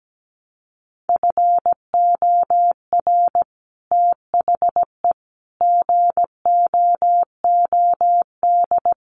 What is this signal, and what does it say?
Morse code, heard as a keyed tone, about 17 wpm, 700 Hz. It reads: FOR THE GOOD
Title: Sets of 5 Words Encore2 17wpm